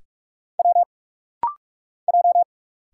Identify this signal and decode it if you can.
Morse code, heard as a keyed tone, about 46 wpm, 700 Hz. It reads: W J
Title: Single Letter-Number - Rapid-Fire 45wpm